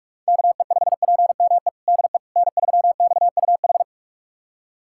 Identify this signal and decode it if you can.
Morse code, heard as a keyed tone, about 47 wpm, 700 Hz. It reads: KE5PME DE N3XUH
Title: USA Callsign DE Callsign 45wpm